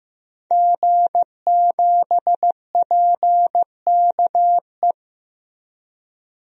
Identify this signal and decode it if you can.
Morse code, heard as a keyed tone, about 15 wpm, 700 Hz. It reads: G7PKE
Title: DXCC-100 Call Signs - 5 Characters - Repeated 3x - 15wpm